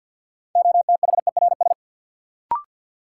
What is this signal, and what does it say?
Morse code, heard as a keyed tone, about 51 wpm, 700 Hz. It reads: OTHERS
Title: Sentences from Top 1500 Words - New Word Drill Alt - 50wpm